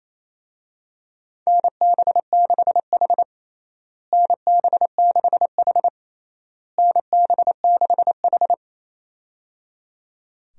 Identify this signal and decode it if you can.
Morse code, heard as a keyed tone, about 27 wpm, 700 Hz. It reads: NB6H NB6H NB6H
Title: USA Call Signs - 4 Characters - Repeated 3x - 28wpm